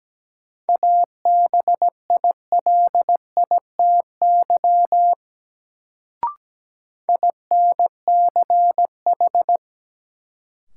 Morse code, heard as a keyed tone, about 17 wpm, 700 Hz. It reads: ABILITY INCH
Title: Sentences from Top 1000 Words - Review Part 3 - New Word Drill - 17wpm